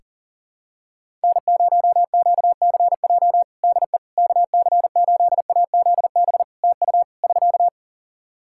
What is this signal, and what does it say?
Morse code, heard as a keyed tone, about 40 wpm, 700 Hz. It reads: N0QCJ DE KC8AZB TU <SK>
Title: Practice QSO - Exchange 4 - Ending - 40wpm